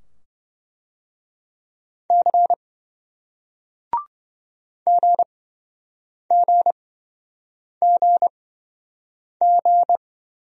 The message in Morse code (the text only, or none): C GGGG